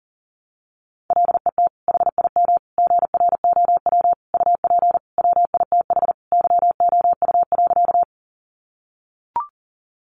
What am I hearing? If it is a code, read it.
Morse code, heard as a keyed tone, about 40 wpm, 700 Hz. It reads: LET HIM GROW UP WITH YOU.
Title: Sentences from Top 300 Words 40wpm